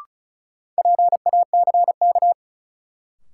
Morse code, heard as a keyed tone, about 34 wpm, 700 Hz. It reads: PACK
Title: Sentences from Top 1300 Words - New Word Drill Alt - 35wpm